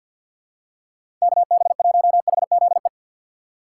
Morse code, heard as a keyed tone, about 51 wpm, 700 Hz. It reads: KB1HZE
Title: USA Call Signs - 6 Characters - 4x Character Spacing - 50wpm